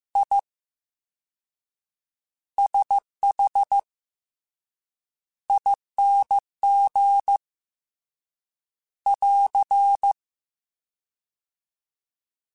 Morse code, heard as a keyed tone, about 15 wpm, spaced as for 7 wpm, 800 Hz. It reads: ISHING <AR>
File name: Special INGs / Special INGs-mp3s / Slow ing @13